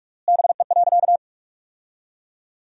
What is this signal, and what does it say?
Morse code, heard as a keyed tone, about 46 wpm, 700 Hz. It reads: BE.
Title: 3x Word-Spacing - Sentences from Top 200 Words - 45wpm